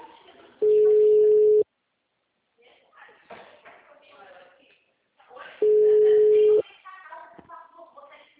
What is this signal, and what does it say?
Ringback tone (Europe, 425 Hz), with speech.
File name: out- -2007- - - .23636